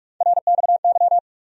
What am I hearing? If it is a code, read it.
Morse code, heard as a keyed tone, about 46 wpm, 700 Hz. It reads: RXY
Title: USA Call Sign Postfixes - 3 Letters 45wpm